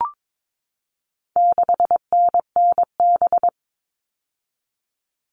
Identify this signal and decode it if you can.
Morse code, heard as a keyed tone, about 22 wpm, 700 Hz. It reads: 6NNB
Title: All Cut Numbers - 4-Digit 22wpm